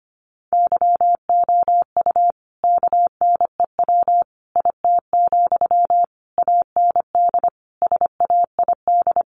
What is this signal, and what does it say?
Morse code, heard as a keyed tone, about 25 wpm, 700 Hz. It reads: YOU KNEW IT, AND SAID
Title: Sentences from Top 500 Words 25wpm